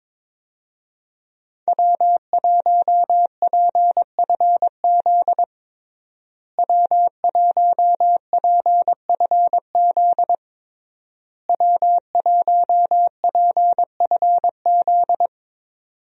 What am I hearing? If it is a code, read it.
Morse code, heard as a keyed tone, about 22 wpm, 700 Hz. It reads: W1PFZ W1PFZ W1PFZ
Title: USA Call Signs - 5 Characters - Repeated 3x - 22wpm